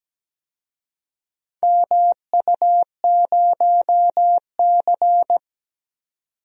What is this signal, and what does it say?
Morse code, heard as a keyed tone, about 17 wpm, 700 Hz. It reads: MU0C
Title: DXCC-100 Call Signs - 4 Characters - 17wpm